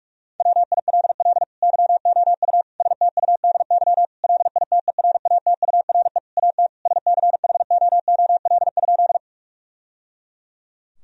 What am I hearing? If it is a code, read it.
Morse code, heard as a keyed tone, about 46 wpm, 700 Hz. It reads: WILL YOU STUDY LITERATURE AT SCHOOL?
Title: Sentences from Top 1000 Words - Review Part 3 - 45wpm